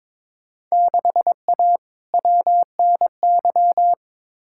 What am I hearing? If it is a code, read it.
Morse code, heard as a keyed tone, about 22 wpm, 700 Hz. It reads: 6A WNY
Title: Field Day Exchanges - 22wpm